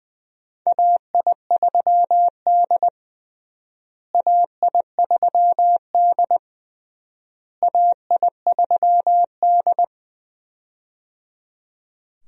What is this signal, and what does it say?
Morse code, heard as a keyed tone, about 21 wpm, 700 Hz. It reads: AI3D AI3D AI3D
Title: USA Call Signs - 4 Characters - Repeated 3x - 20wpm